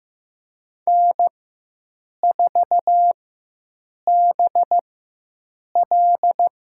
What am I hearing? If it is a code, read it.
Morse code, heard as a keyed tone, about 15 wpm, spaced as for 7 wpm, 700 Hz. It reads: N4BL
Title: USA Call Signs - 4 Characters - 4x Character Spacing - 15wpm